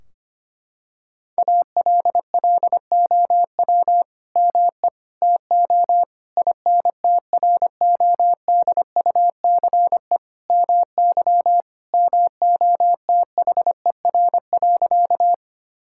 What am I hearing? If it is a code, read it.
Morse code, heard as a keyed tone, about 25 wpm, 700 Hz. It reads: ALLOW ME TO INTRODUCE MY MOTHER.
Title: Sentences from Top 1600 Words 25wpm